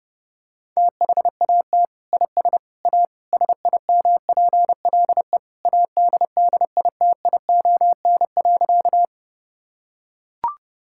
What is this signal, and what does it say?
Morse code, heard as a keyed tone, about 30 wpm, 700 Hz. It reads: THAT IS A SIMPLE ADDITION.
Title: Sentences from Top 1000 Words - Review Part 6 - 30wpm